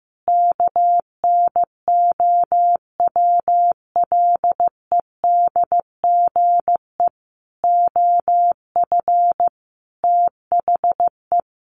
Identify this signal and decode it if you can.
Morse code, heard as a keyed tone, about 15 wpm, 700 Hz. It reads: KNOWLEDGE OF THE
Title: Sentences from Top 1500 Words 15wpm